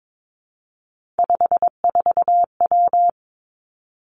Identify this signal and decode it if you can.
Morse code, heard as a keyed tone, about 22 wpm, 700 Hz. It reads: 54W
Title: Three Challenging Letter-Number Combinations 22wpm